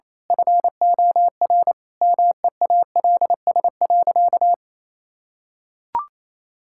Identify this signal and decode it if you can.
Morse code, heard as a keyed tone, about 28 wpm, 700 Hz. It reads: FOR MEALS.
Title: Pangram Challenge v2 - 28wpm